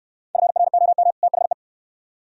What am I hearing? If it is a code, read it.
Morse code, heard as a keyed tone, about 65 wpm, 690 Hz. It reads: FROM THE